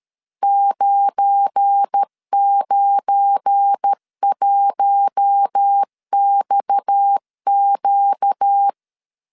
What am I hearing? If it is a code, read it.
Morse code, heard as a keyed tone, about 13 wpm, 800 Hz. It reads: 991XQ
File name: LEZIONI / 7 GF9Z / 7 GF9 12wpm 2319